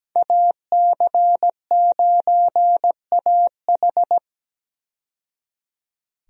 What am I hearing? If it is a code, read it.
Morse code, heard as a keyed tone, about 17 wpm, 700 Hz. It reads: AC9AH
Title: USA Call Signs - 5 Characters - 17wpm